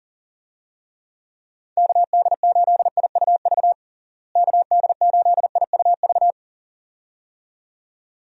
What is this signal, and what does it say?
Morse code, heard as a keyed tone, about 37 wpm, 700 Hz. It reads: KD8IUV KD8IUV
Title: USA Call Signs - 6 Characters - Repeated 2x - 40wpm